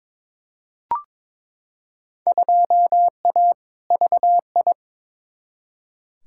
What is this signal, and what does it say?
Morse code, heard as a keyed tone, about 21 wpm, 700 Hz. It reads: 2A VI
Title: Field Day Exchanges - 22wpm